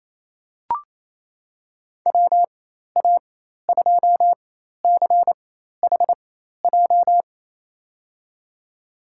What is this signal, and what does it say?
Morse code, heard as a keyed tone, about 28 wpm, spaced as for 13 wpm, 700 Hz. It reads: WA2CHJ
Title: USA Call Signs - 6 Characters - 4x Character Spacing - 28wpm